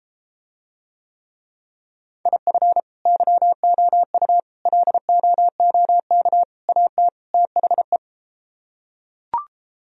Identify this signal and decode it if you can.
Morse code, heard as a keyed tone, about 34 wpm, 700 Hz. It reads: IF YOU LOOK AT THE